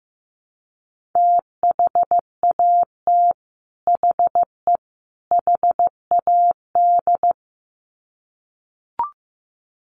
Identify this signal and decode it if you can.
Morse code, heard as a keyed tone, about 15 wpm, 700 Hz. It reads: THAT HE HAD
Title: Sets of 3 Words 15wpm